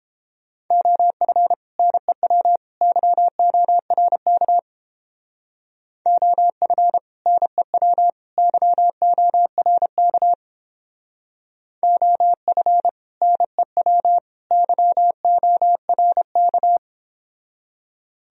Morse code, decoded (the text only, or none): OF NEW YORK OF NEW YORK OF NEW YORK